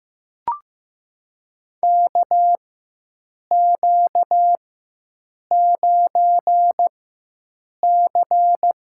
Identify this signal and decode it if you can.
Morse code, heard as a keyed tone, about 15 wpm, spaced as for 7 wpm, 700 Hz. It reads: KQ9C
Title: USA Call Signs - 4 Characters - 4x Character Spacing - 15wpm